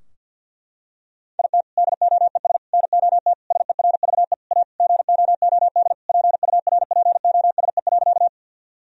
Morse code, heard as a keyed tone, about 50 wpm, 700 Hz. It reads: IT DOES NOT SERVE A GOOD PURPOSE.
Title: Sentences from Top 1000 Words - Review Part 2 - 50wpm